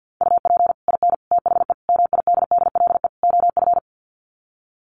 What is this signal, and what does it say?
Morse code, heard as a keyed tone, about 48 wpm, 700 Hz. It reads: UP IN THE MIDDLE OF
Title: Sets of 5 Words Encore 50wpm